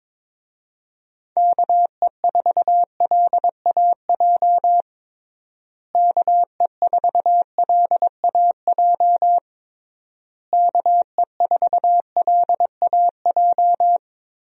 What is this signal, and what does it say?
Morse code, heard as a keyed tone, about 22 wpm, 700 Hz. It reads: KE4LAJ KE4LAJ KE4LAJ
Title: USA Call Signs - 6 Characters - Repeated 3x - 22wpm